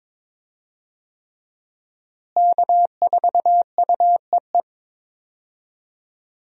Morse code, heard as a keyed tone, about 21 wpm, 700 Hz. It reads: K4UEE
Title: USA Call Signs - 5 Characters - Repeated 3x - 22wpm